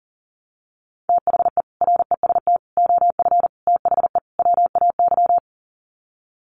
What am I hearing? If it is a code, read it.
Morse code, heard as a keyed tone, about 40 wpm, 700 Hz. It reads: THE REST OF THE WAY
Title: Sets of 5 Words Encore 40wpm